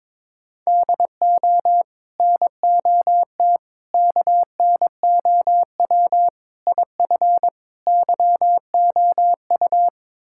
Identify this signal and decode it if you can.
Morse code, heard as a keyed tone, about 22 wpm, 700 Hz. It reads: DO NOT KNOW IF YOU
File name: practice / 22wpm / Build A Phrase - 5-Word Phrases - 22wpm